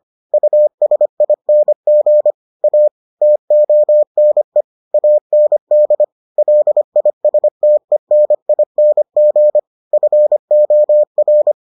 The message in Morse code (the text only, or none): USING A TONE AND LISTENING FOR